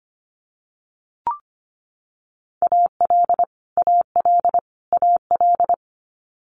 Morse code, heard as a keyed tone, about 25 wpm, 700 Hz. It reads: AL AL AL
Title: Multi-letter Phonemes Repeated 3x 25wpm